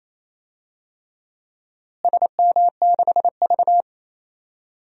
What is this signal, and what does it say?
Morse code, heard as a keyed tone, about 28 wpm, 700 Hz. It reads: SM6V
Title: DXCC-100 Call Signs - 4 Characters - Repeated 3x - 28wpm